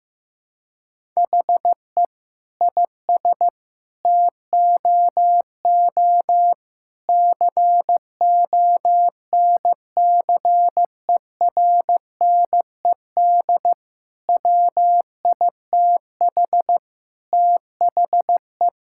Morse code, heard as a keyed tone, about 15 wpm, 700 Hz. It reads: HE IS TOO CONCERNED WITH THE
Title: Sentences from Top 1000 Words - Review Part 6 - 15wpm